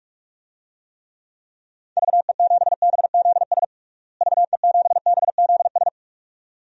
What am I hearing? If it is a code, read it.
Morse code, heard as a keyed tone, about 43 wpm, 700 Hz. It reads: VE7BZS VE7BZS
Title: DXCC-100 Call Signs - 6 Characters - Repeated 2x - 45wpm